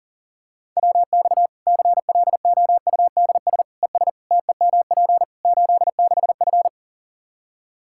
Morse code, heard as a keyed tone, about 37 wpm, 700 Hz. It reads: WX CLOUDS ES TEMP 86F
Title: Practice QSO - Exchange 2 - Info - 40wpm